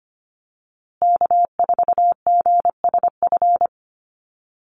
Morse code, heard as a keyed tone, about 25 wpm, 700 Hz. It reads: K4GSF
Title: USA Call Signs Repeated 2x 25wpm